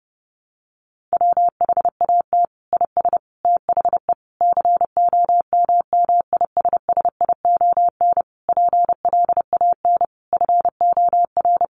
Morse code, heard as a keyed tone, about 30 wpm, 700 Hz. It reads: WHAT IS THE COMMISSION PLAN FOR
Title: Sentences from Top 1400 Words 30wpm